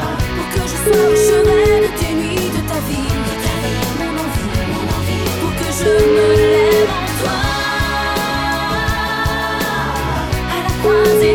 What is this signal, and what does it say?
Ringback tone (Europe, 425 Hz), with noise.